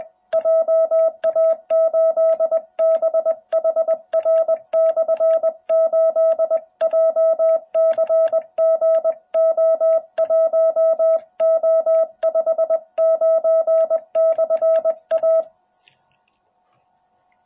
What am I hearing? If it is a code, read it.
Morse code, heard as a keyed tone, about 20 wpm, 650 Hz. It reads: JA8BHR/8 JCGO1O59/A